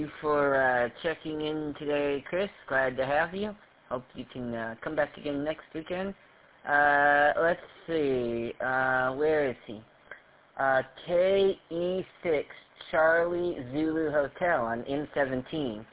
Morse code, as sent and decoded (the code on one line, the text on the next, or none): none